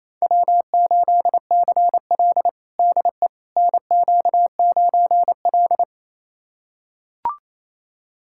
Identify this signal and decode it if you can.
Morse code, heard as a keyed tone, about 27 wpm, 700 Hz. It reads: W8CL DE NQ9L
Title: USA Callsign DE Callsign - 4 Characters - 28wpm